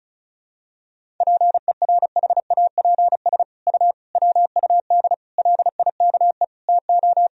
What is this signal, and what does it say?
Morse code, heard as a keyed tone, about 36 wpm, 700 Hz. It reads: PERHAPS U WUD LIKE TO
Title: Sentences from Top 500 Words and Abbreviations 35wpm